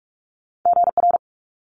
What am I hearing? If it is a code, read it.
Morse code, heard as a keyed tone, about 46 wpm, 700 Hz. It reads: GR
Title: Multi-letter Phonemes Quick Fire 45wpm